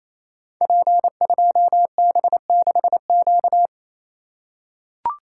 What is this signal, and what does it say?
Morse code, heard as a keyed tone, about 27 wpm, 700 Hz. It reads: P2B6Q
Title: Code Groups - 5 Letter:Numbers - 28wpm